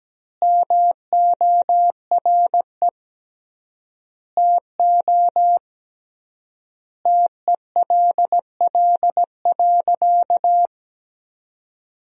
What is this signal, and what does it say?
Morse code, heard as a keyed tone, about 17 wpm, 700 Hz. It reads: MORE TO TELL.
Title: 3x Word-Spacing - Sentences from Top 200 Words - 17wpm